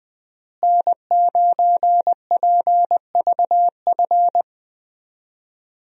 Morse code, heard as a keyed tone, about 20 wpm, 700 Hz. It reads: N9PVF
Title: USA Call Signs - 5 Characters - Repeated 2x - 20wpm